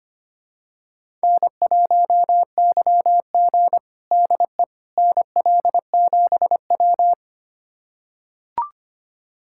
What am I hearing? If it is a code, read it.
Morse code, heard as a keyed tone, about 25 wpm, 700 Hz. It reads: N1YG DE NL7W
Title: USA Callsign DE Callsign - 4 Characters - 25wpm